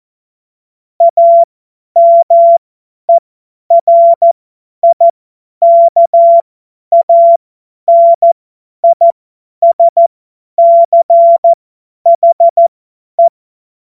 Morse code, heard as a keyed tone, about 14 wpm, spaced as for 10 wpm, 680 Hz. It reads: AMERIKANISCHE
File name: Morsekurs / Witze / Witz 12 Dezember 2023 50 F70